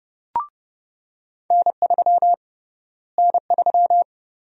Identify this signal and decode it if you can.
Morse code, heard as a keyed tone, about 29 wpm, 700 Hz. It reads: N3 N3
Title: USA Call Sign Prefixes - Repeated 2x - 30wpm